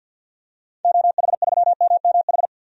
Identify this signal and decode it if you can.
Morse code, heard as a keyed tone, about 47 wpm, 700 Hz. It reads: OH3MMH
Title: DXCC-100 Call Signs - 6 Characters - 50wpm